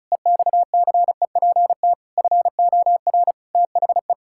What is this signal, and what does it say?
Morse code, heard as a keyed tone, about 36 wpm, 700 Hz. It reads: EXCEPT FOR THE
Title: Pangram Challenge v2 - 35wpm